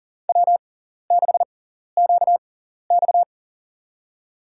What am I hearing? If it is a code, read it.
Morse code, heard as a keyed tone, about 37 wpm, spaced as for 13 wpm, 700 Hz. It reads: W6QX
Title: USA Call Signs - 4 Characters - 6x Character Spacing - 40wpm